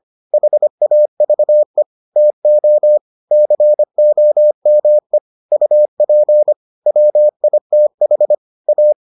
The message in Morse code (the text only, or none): HAVE TO COME UP WITH A